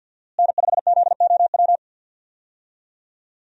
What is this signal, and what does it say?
Morse code, heard as a keyed tone, about 45 wpm, 700 Hz. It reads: N5ZOW
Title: USA Call Signs - 5 Characters - Repeated 2x - 50wpm